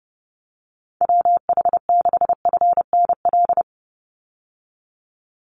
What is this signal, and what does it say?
Morse code, heard as a keyed tone, about 30 wpm, 700 Hz. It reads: WH6FNL
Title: USA Call Signs 30wpm